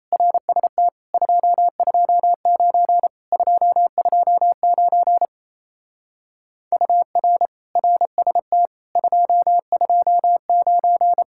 Morse code, heard as a keyed tone, about 37 wpm, 700 Hz. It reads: RST 229 229 UR RST 229